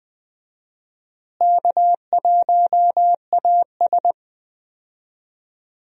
Morse code, heard as a keyed tone, about 20 wpm, 700 Hz. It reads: K1AS
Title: USA Call Signs - 4 Characters - Repeated 3x - 20wpm